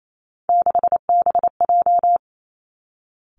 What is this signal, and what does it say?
Morse code, heard as a keyed tone, about 28 wpm, 700 Hz. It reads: 6BJ
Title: Three Challenging Letter-Number Combinations 28wpm